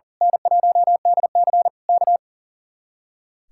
Morse code, heard as a keyed tone, about 38 wpm, 700 Hz. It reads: N1DC K
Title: Speed-Racing Common QSO Phrases - 40wpm